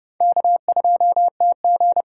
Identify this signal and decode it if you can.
Morse code, heard as a keyed tone, about 30 wpm, 690 Hz. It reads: K2TG
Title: USA Call Signs - 4x Character Spacing - 30wpm